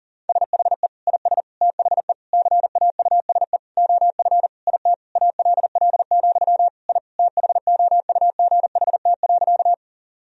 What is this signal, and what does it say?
Morse code, heard as a keyed tone, about 40 wpm, 700 Hz. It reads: SHE IS THE CAUSE OF IT ALL, I THOUGHT.
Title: Sentences from Top 1000 Words - Review Part 1 - 40wpm